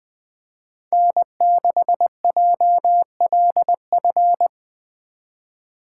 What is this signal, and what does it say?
Morse code, heard as a keyed tone, about 21 wpm, 700 Hz. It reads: N6JLF
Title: USA Call Signs - 5 Characters - 3x Character Spacing - 20wpm